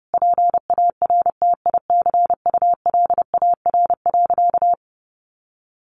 Morse code, heard as a keyed tone, about 30 wpm, 700 Hz. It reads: PARTICULAR.
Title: Sentences from Top 600 Words 30wpm